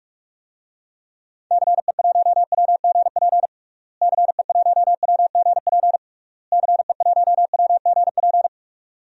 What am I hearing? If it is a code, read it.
Morse code, heard as a keyed tone, about 46 wpm, 700 Hz. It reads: CE1WGP CE1WGP CE1WGP
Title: DXCC-100 Call Signs - 6 Characters - Repeated 3x - 45wpm